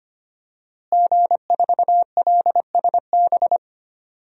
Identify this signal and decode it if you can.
Morse code, heard as a keyed tone, about 25 wpm, 700 Hz. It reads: G4LSB
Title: DXCC-100 Call Signs - 5 Characters - Repeated 3x - 25wpm